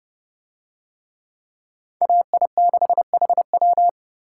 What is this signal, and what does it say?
Morse code, heard as a keyed tone, about 29 wpm, 700 Hz. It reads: AI6HW